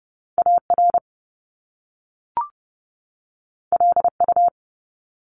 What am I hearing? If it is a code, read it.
Morse code, heard as a keyed tone, about 30 wpm, 700 Hz. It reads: AR LU
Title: Top Two Letter Combinations in Words 30wpm